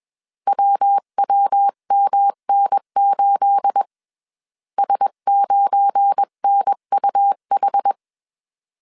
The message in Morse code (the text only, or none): WWMN8 S9NUH